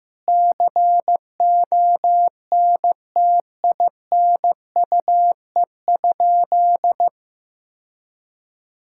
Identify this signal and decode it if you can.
Morse code, heard as a keyed tone, about 15 wpm, 700 Hz. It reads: CONTINUE?
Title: Sentences from Top 1000 Words - Review Part 6 - 15wpm